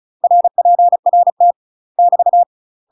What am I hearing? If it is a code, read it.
Morse code, heard as a keyed tone, about 37 wpm, 700 Hz. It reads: RPRT <BT>
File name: UK MASTER QSOs TRAINING / 35WPM / UK MASTER QSOs 0004